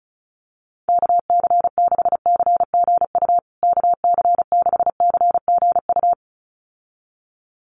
Morse code, heard as a keyed tone, about 34 wpm, 700 Hz. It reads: KC6CGU KC6CGU
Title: USA Call Signs Repeated 2x 35wpm